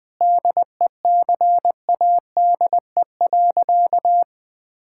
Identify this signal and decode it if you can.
Morse code, heard as a keyed tone, about 20 wpm, 700 Hz. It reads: DECADE.
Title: Sentences with 6-Word Phrases - 20wpm